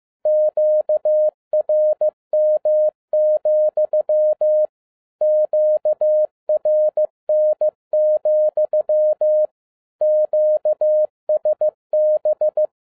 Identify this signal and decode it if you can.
Morse code, heard as a keyed tone, about 15 wpm, 610 Hz. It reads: QRM, QRN, QSB
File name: MY CW LIFE 1 15wpm